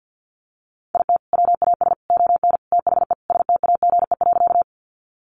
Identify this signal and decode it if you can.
Morse code, heard as a keyed tone, about 50 wpm, 700 Hz. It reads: IT WAS ON THE STAGE.
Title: Sentences from Top 1200 Words 50wpm